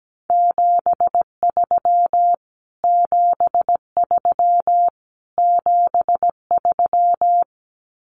Morse code, heard as a keyed tone, about 17 wpm, 700 Hz. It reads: 73 73 73
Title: Abbreviations Repeated 3x - 17wpm